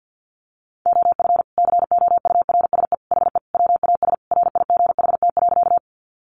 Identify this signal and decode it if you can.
Morse code, heard as a keyed tone, about 50 wpm, 700 Hz. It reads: OF COURSE HE WAS RIGHT.
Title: Sentences from Top 500 Words 50wpm